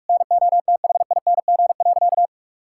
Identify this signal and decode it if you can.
Morse code, heard as a keyed tone, about 47 wpm, 700 Hz. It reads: NOTHING.
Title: Sentences from Top 1000 Words - Review Part 5 - 45wpm